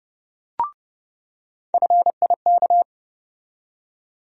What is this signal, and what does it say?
Morse code, heard as a keyed tone, about 31 wpm, 700 Hz. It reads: FIK
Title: USA Call Sign Postfixes - 3 Letters 30wpm